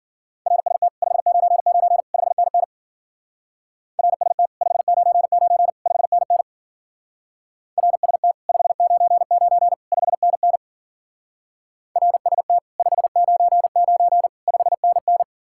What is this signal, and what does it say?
Morse code, heard as a keyed tone, about 50 wpm, 700 Hz. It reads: RST 599 5NN RST 599 5NN RST 599 5NN RST 599 5NN